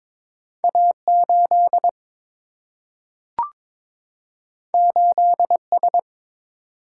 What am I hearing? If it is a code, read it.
Morse code, heard as a keyed tone, about 22 wpm, 700 Hz. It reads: A8 8S
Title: Single-Letter Single-Number 22wpm